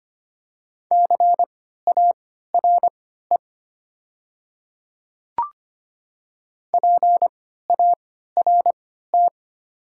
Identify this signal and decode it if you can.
Morse code, heard as a keyed tone, about 25 wpm, spaced as for 14 wpm, 700 Hz. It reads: CARE PART
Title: 3x Character Spacing - Top 300 Words - 25wpm